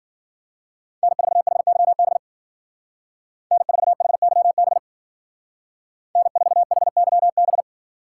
Morse code, heard as a keyed tone, about 48 wpm, 700 Hz. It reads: N4HYB N4HYB N4HYB
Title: Speed Racing - USA 1x3 Call Signs 40wpm